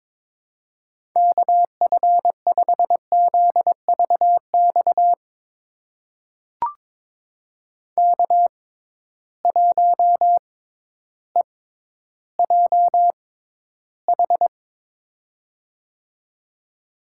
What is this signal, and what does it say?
Morse code, heard as a keyed tone, about 21 wpm, spaced as for 12 wpm, 700 Hz. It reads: KF5ZVX K1EJH
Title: USA Call Signs - 6x Character Spacing - 22wpm